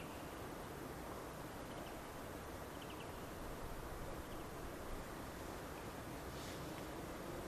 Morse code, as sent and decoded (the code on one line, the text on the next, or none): none